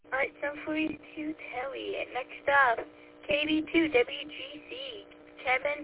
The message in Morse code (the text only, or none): none